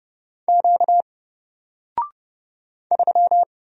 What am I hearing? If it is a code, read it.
Morse code, heard as a keyed tone, about 30 wpm, 700 Hz. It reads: Q 3
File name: Single-Letter-Number-30wpm